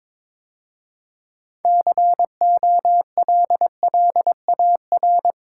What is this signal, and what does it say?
Morse code, heard as a keyed tone, about 25 wpm, 700 Hz. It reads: COLLAR
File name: Easy Words - Speed-Racing with Hints - 22wpm